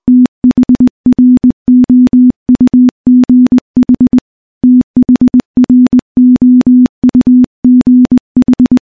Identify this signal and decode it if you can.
Morse code, heard as a keyed tone, about 19 wpm, 250 Hz. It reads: THROUGH THROUGH
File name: morse / morse-training-20WPM